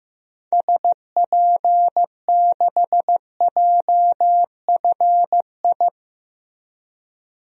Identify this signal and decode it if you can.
Morse code, heard as a keyed tone, about 15 wpm, 700 Hz. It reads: SP6JFI
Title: DXCC-100 Call Signs - 6 Characters - Repeated 2x - 15wpm